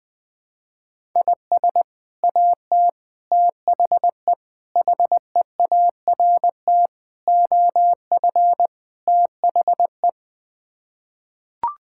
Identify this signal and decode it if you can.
Morse code, heard as a keyed tone, about 20 wpm, 700 Hz. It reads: IS AT THE HEART OF THE
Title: Sets of 6 Words - 20wpm